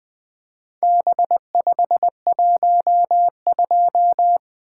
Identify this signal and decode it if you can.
Morse code, heard as a keyed tone, about 20 wpm, 700 Hz. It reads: B512
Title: Four Letter-Number Combinations - 20wpm